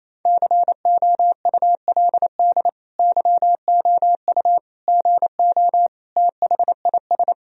Morse code, heard as a keyed tone, about 28 wpm, 700 Hz. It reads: COULD YOU GO THIS
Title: Sentences from Top 1000 Words - Review Part 4 - 28wpm